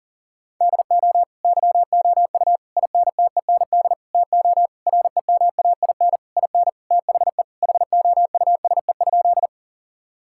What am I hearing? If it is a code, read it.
Morse code, heard as a keyed tone, about 37 wpm, 700 Hz. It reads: DO YOU INTEND TO REMAIN IN THE HOUSE?
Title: Sentences from Top 1600 Words 40wpm